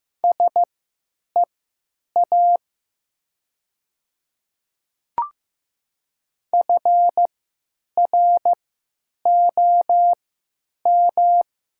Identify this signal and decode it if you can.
Morse code, heard as a keyed tone, about 15 wpm, spaced as for 9 wpm, 700 Hz. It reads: SEA FROM
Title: 3x Character Spacing - Top 500 Words - 15wpm